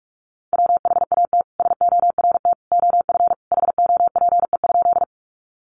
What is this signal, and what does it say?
Morse code, heard as a keyed tone, about 44 wpm, 700 Hz. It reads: WHAT SORT OF HOPE?
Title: Sentences from Top 1300 Words 45wpm